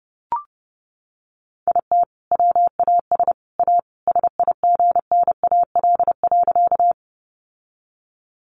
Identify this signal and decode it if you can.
Morse code, heard as a keyed tone, about 30 wpm, 700 Hz. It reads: IT WAS A SIGNAL.
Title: Sentences from Top 1400 Words 30wpm